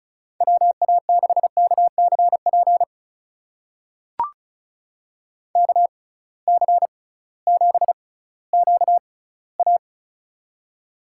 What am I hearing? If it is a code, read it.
Morse code, heard as a keyed tone, about 36 wpm, spaced as for 19 wpm, 700 Hz. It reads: WA6KCP KC7QA
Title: USA Call Signs - 6x Character Spacing - 35wpm